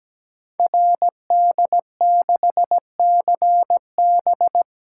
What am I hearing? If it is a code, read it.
Morse code, heard as a keyed tone, about 17 wpm, 700 Hz. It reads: RD6CB
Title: Code Groups - 5 Letter:Numbers - 17wpm